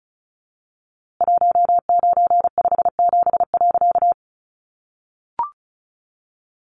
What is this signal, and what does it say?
Morse code, heard as a keyed tone, about 34 wpm, 700 Hz. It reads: 1957.
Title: A Sound Of Thunder 35wpm